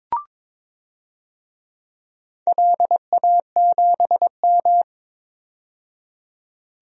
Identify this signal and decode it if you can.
Morse code, heard as a keyed tone, about 22 wpm, 700 Hz. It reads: LA7M
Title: DXCC-100 Call Signs - 4 Characters - 22wpm